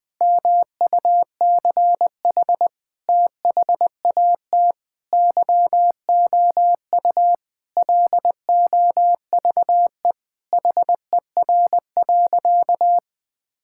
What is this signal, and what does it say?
Morse code, heard as a keyed tone, about 21 wpm, 700 Hz. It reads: MUCH THAT YOU LOVE HER.
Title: Sentences from Top 1600 Words 20wpm